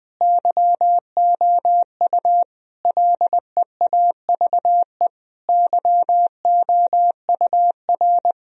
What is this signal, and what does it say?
Morse code, heard as a keyed tone, about 20 wpm, 700 Hz. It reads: YOU LEAVE YOUR
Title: Sentences from Top 1000 Words - Review Part 3 - 20wpm